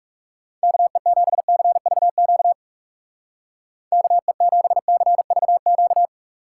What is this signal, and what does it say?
Morse code, heard as a keyed tone, about 44 wpm, 700 Hz. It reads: KE7CVQ KE7CVQ